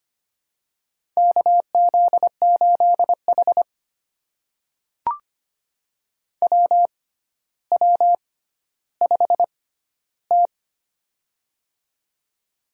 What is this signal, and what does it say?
Morse code, heard as a keyed tone, about 25 wpm, spaced as for 13 wpm, 700 Hz. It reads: KZ8H WW5T
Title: USA Call Signs - 4 Characters - 6x Character Spacing - 25wpm